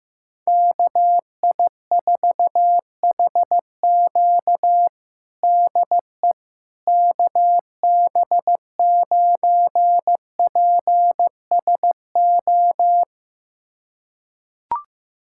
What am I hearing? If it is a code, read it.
Morse code, heard as a keyed tone, about 16 wpm, 700 Hz. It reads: KI4HQ DE KB9PSO
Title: USA Callsign DE Callsign 15wpm